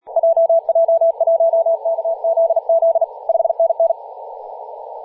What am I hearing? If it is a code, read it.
Morse code, heard as a keyed tone, about 38 wpm, 660 Hz, with radio noise. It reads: JJ1KZZ 5NN
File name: posts / 2016-aug-26-2236z-yj0cow-18-073mhz-cw